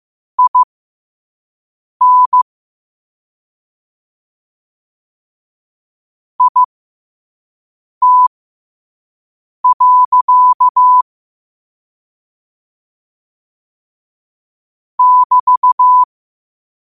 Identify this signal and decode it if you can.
Morse code, heard as a keyed tone, about 15 wpm, spaced as for 5 wpm, 1000 Hz. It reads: IN IT. <BT>
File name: qotdfiles / 5wpm